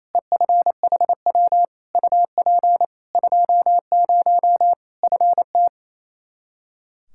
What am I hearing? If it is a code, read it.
Morse code, heard as a keyed tone, about 28 wpm, 700 Hz. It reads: EFHW UP 20 FT
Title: Practice QSO - Exchange 2 - Info - 28wpm